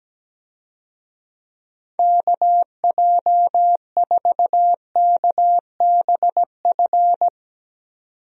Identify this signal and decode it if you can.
Morse code, heard as a keyed tone, about 17 wpm, 700 Hz. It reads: KJ4KBF
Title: USA Call Signs - 6 Characters - 17wpm